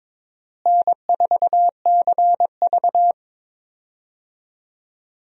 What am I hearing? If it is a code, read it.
Morse code, heard as a keyed tone, about 22 wpm, 700 Hz. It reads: N4CV
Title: USA Call Signs - 4 Characters - Repeated 3x - 22wpm